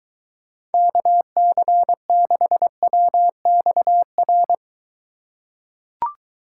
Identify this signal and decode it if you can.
Morse code, heard as a keyed tone, about 23 wpm, 700 Hz. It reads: KC6WXR